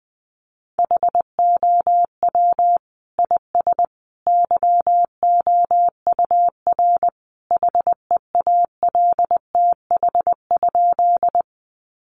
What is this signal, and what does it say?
Morse code, heard as a keyed tone, about 20 wpm, 700 Hz. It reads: HOW IS YOUR HEALTH?
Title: Sentences from Top 600 Words 20wpm